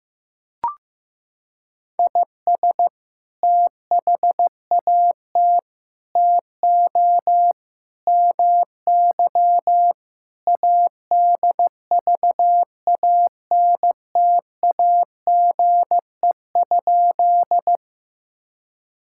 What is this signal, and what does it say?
Morse code, heard as a keyed tone, about 14 wpm, 700 Hz. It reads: IS THAT TO MY ADVANTAGE?
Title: Sentences from Top 1000 Words - Review Part 6 - 15wpm